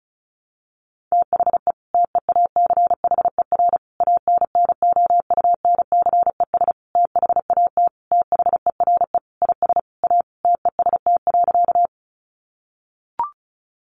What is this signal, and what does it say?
Morse code, heard as a keyed tone, about 34 wpm, 700 Hz. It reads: THE TEACHER ANNOUNCES THAT THERE IS A TEST.
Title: Additional Easy Phrases 35wpm